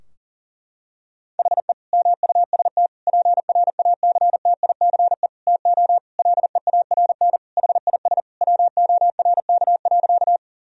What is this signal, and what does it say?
Morse code, heard as a keyed tone, about 40 wpm, 700 Hz. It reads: HE MUST PRACTICE TO LEARN HIS WORK.
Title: Sentences from Top 1000 Words - Review Part 4 - 40wpm